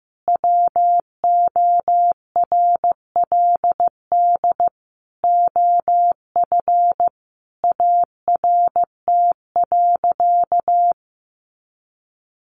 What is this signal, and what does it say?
Morse code, heard as a keyed tone, about 15 wpm, 700 Hz. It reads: WORLD OF ART.
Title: Sentences from Top 1500 Words 15wpm